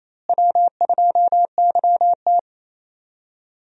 Morse code, heard as a keyed tone, about 28 wpm, 700 Hz. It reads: W2YT
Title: USA Call Signs - 4 Characters - Repeated 3x - 28wpm